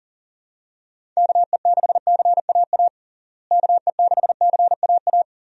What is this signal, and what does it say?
Morse code, heard as a keyed tone, about 37 wpm, 700 Hz. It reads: KE6CAA KE6CAA
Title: USA Call Signs - 6 Characters - Repeated 3x - 40wpm